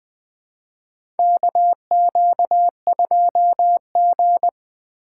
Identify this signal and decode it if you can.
Morse code, heard as a keyed tone, about 20 wpm, 700 Hz. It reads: KQ2G
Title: USA Call Signs - 4 Characters - 3x Character Spacing - 20wpm